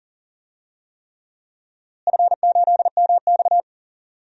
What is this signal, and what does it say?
Morse code, heard as a keyed tone, about 40 wpm, 700 Hz. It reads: F8MX
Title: DXCC-100 Call Signs - 4 Characters - Repeated 3x - 40wpm